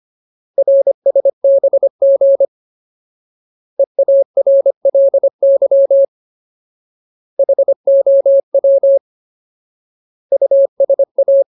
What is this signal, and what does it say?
Morse code, heard as a keyed tone, about 25 wpm, 550 Hz. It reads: RSBG EARLY HOW USA
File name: Ultimate Word List / 25WPM Ultimate